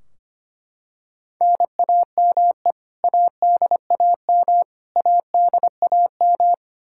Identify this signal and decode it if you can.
Morse code, heard as a keyed tone, about 25 wpm, 700 Hz. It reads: NAME ADAM ADAM
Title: K1USN SST QSO Practice 25wpm